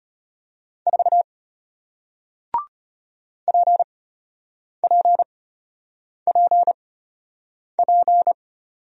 Morse code, heard as a keyed tone, about 33 wpm, spaced as for 8 wpm, 700 Hz. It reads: 4 PPPP